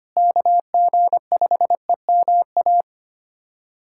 Morse code, heard as a keyed tone, about 25 wpm, 700 Hz. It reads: KG5EMA
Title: USA Callsign DE Callsign - 6 Characters - 25wpm